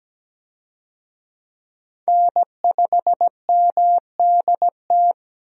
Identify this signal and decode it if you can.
Morse code, heard as a keyed tone, about 17 wpm, 700 Hz. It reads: N5MDT
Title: USA Call Signs - 5 Characters - 17wpm